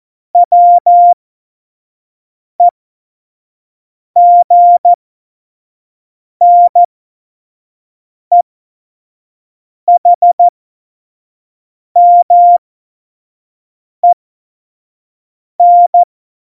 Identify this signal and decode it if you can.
Morse code, heard as a keyed tone, about 14 wpm, spaced as for 5 wpm, 700 Hz. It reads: WEGNEHMEN
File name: Morsekurs / Witze / Witz 26 Mai 2020, 25 F70